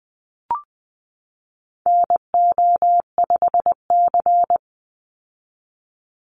Morse code, heard as a keyed tone, about 20 wpm, 700 Hz. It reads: NO5C
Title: USA Call Signs 20wpm